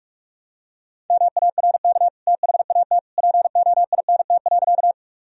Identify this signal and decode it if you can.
Morse code, heard as a keyed tone, about 46 wpm, 700 Hz. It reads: MARK THAT POINT.
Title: Sentences from Top 1000 Words - Review Part 1 - 45wpm